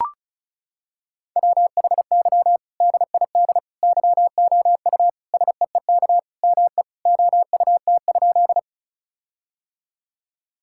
Morse code, heard as a keyed tone, about 36 wpm, 700 Hz. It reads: WHY DID YOU SEEK ME OUT?
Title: Sentences from Top 1600 Words 35wpm